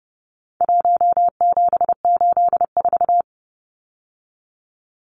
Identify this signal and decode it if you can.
Morse code, heard as a keyed tone, about 30 wpm, 700 Hz. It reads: 1784
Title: Numbers - 4-Digit 30wpm